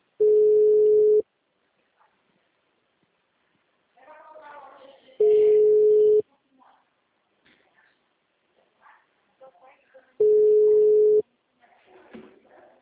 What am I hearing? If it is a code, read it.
Ringback tone (Europe, 425 Hz), with speech.
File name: out- -2005- - - .80377